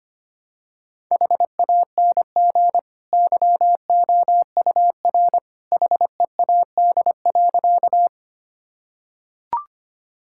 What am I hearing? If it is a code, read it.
Morse code, heard as a keyed tone, about 25 wpm, 700 Hz. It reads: HANG YOUR HEAD.
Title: Sentences from Top 1600 Words 25wpm